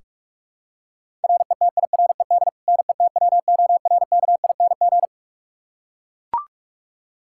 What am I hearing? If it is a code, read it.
Morse code, heard as a keyed tone, about 42 wpm, 700 Hz. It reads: RETIRED NETWORKING
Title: Practice QSO - Exchange 3 - More Info - 45wpm